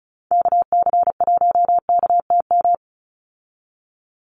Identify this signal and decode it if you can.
Morse code, heard as a keyed tone, about 34 wpm, 700 Hz. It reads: KC1KTM
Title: USA Call Signs 35wpm